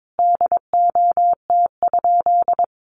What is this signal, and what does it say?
Morse code, heard as a keyed tone, about 22 wpm, 700 Hz. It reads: DOT?
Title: Sentences from Top 1500 Words 22wpm